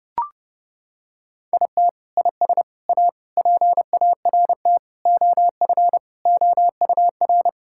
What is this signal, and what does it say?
Morse code, heard as a keyed tone, about 30 wpm, 700 Hz. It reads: IT IS A PART OF OUR
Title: Sentences from Top 1000 Words - Review Part 5 - 30wpm